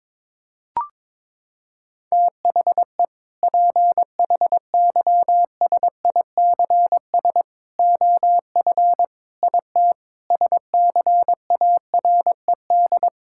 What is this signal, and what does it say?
Morse code, heard as a keyed tone, about 21 wpm, 700 Hz. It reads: THE PHYSICS OF IT SCARED
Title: Sentences from Top 1000 Words - Review Part 5 - 22wpm